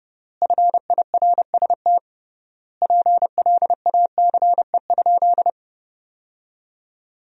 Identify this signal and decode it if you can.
Morse code, heard as a keyed tone, about 32 wpm, 700 Hz. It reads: FIRST PLACE?
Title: 3x Word-Spacing - Sentences from Top 200 Words - 30wpm